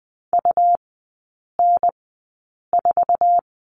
Morse code, heard as a keyed tone, about 20 wpm, spaced as for 8 wpm, 700 Hz. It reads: UN4
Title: 205 Lesson 33e - Letter j - Three Characters 20wpm